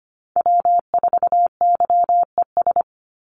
Morse code, heard as a keyed tone, about 25 wpm, 700 Hz. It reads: W4YES
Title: Call Signs - 25wpm